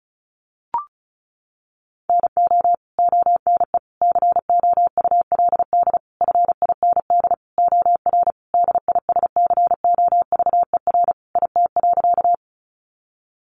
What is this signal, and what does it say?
Morse code, heard as a keyed tone, about 34 wpm, 700 Hz. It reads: NO ONE COULD FIND OR DISCOVER IT.
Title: Sentences from Top 1100 Words 35wpm